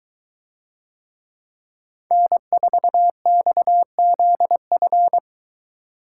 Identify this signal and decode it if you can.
Morse code, heard as a keyed tone, about 21 wpm, 700 Hz. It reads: N4XZF